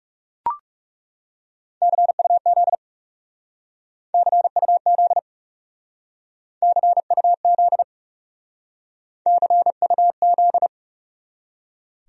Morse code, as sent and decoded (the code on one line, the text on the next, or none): -.-. ..- --.. / -.-. ..- --.. / -.-. ..- --.. / -.-. ..- --..
CUZ CUZ CUZ CUZ